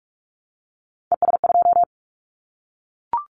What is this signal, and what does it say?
Morse code, heard as a keyed tone, about 44 wpm, 700 Hz. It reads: ES2
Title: Callsign Prefixes for Top 100 DXCC - 45wpm